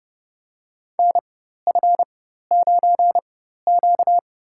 Morse code, heard as a keyed tone, about 30 wpm, spaced as for 14 wpm, 700 Hz. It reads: NF9Q
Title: USA Call Signs - 4x Character Spacing - 30wpm